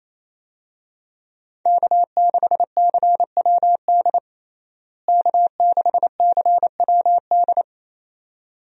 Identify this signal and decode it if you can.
Morse code, heard as a keyed tone, about 28 wpm, 700 Hz. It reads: K6CWD K6CWD
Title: USA Call Signs - 5 Characters - Repeated 2x - 28wpm